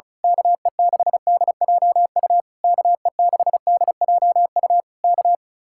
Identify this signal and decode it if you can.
Morse code, heard as a keyed tone, about 32 wpm, 700 Hz. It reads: KE6DJU KE6DJU K
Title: Speed-Racing Common QSO Phrases Encore - 35wpm